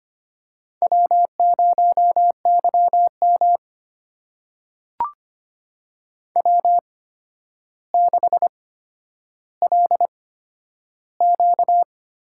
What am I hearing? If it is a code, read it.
Morse code, heard as a keyed tone, about 25 wpm, spaced as for 11 wpm, 700 Hz. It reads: W0YM W6LQ
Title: USA Call Signs - 4 Characters - 8x Character Spacing - 25wpm